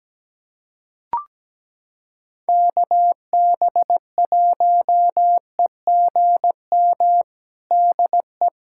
Morse code, heard as a keyed tone, about 17 wpm, 700 Hz. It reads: KB1EGM DE
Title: USA Callsign DE Callsign 17wpm